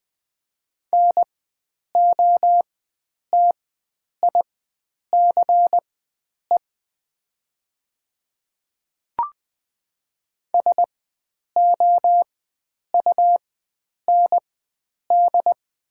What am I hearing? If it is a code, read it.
Morse code, heard as a keyed tone, about 20 wpm, spaced as for 9 wpm, 700 Hz. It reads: NOTICE SOUND
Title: 4x Character Spacing - Top 500 Words - 20wpm